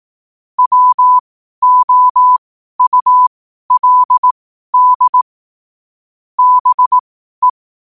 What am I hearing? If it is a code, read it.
Morse code, heard as a keyed tone, about 18 wpm, spaced as for 12 wpm, 1000 Hz. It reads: WOULD BE
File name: qotdfiles / 12wpm